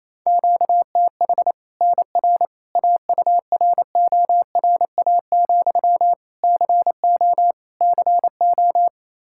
Morse code, heard as a keyed tone, about 28 wpm, 700 Hz. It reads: QTH NR AURORA, CO CO
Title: Practice QSO - Exchange 1 - Beginning - 28wpm